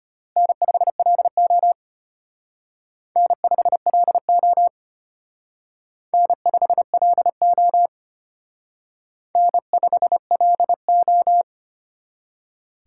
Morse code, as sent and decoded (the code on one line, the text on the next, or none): -. ..... .-.. --- / -. ..... .-.. --- / -. ..... .-.. --- / -. ..... .-.. ---
N5LO N5LO N5LO N5LO